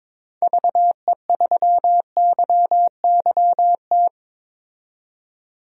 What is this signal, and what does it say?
Morse code, heard as a keyed tone, about 22 wpm, 700 Hz. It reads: VE3YYT
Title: DXCC-100 Call Signs - 6 Characters - 22wpm